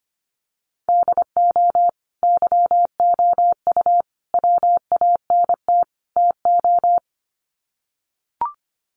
Morse code, heard as a keyed tone, about 25 wpm, 700 Hz. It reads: DO YOU WANT TO
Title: Sets of 4 Words 25wpm